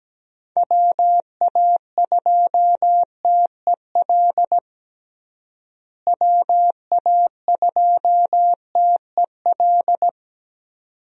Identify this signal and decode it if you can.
Morse code, heard as a keyed tone, about 17 wpm, 700 Hz. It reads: WA2TEL WA2TEL
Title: USA Call Signs - 6 Characters - Repeated 2x - 17wpm